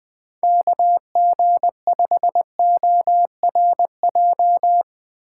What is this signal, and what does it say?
Morse code, heard as a keyed tone, about 20 wpm, 700 Hz. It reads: KG5ORJ
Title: USA Call Signs - 6 Characters - 2x Character Spacing - 20wpm